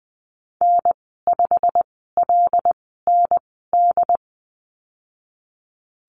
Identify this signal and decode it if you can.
Morse code, heard as a keyed tone, about 20 wpm, spaced as for 15 wpm, 700 Hz. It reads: N5LND
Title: 147 Lesson 24g - Letter y - Call Signs 20wpm